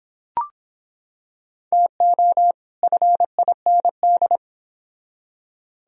Morse code, heard as a keyed tone, about 25 wpm, 700 Hz. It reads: TO FIND